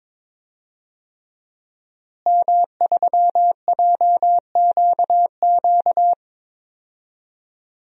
Morse code, heard as a keyed tone, about 22 wpm, 700 Hz. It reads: M3JQQ
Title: DXCC-100 Call Signs - 5 Characters - Repeated 3x - 22wpm